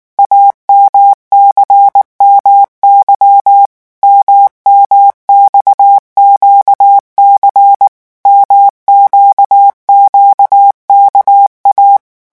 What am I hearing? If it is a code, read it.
Morse code, heard as a keyed tone, about 19 wpm, 800 Hz. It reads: AMCMY MMXQC MQQKA
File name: morse mp3 / revision2-vitesse 18 mots